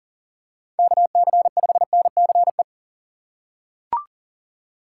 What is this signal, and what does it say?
Morse code, heard as a keyed tone, about 38 wpm, 700 Hz. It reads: KC5NCE
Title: USA Call Signs - 6 Characters - 6x Character Spacing - 40wpm